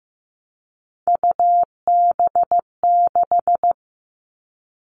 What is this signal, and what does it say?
Morse code, heard as a keyed tone, about 15 wpm, 700 Hz. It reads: UB6
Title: All Cut Numbers - 3-Digit 15wpm